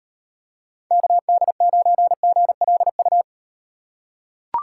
Morse code, heard as a keyed tone, about 38 wpm, 700 Hz. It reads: KD9GLU